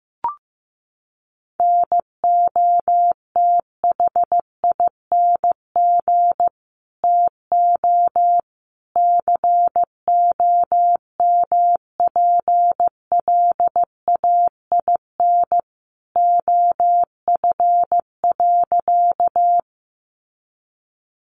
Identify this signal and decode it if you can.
Morse code, heard as a keyed tone, about 15 wpm, 700 Hz. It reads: NOTHING TO COMPLAIN OF.
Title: Sentences from Top 1300 Words 15wpm